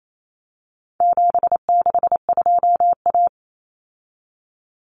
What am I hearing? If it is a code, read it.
Morse code, heard as a keyed tone, about 28 wpm, 700 Hz. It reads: 762A
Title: Common Cut Numbers - 4-Digit 28wpm